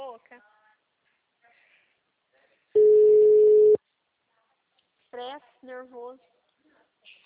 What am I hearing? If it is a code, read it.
Ringback tone (Europe, 425 Hz), with speech.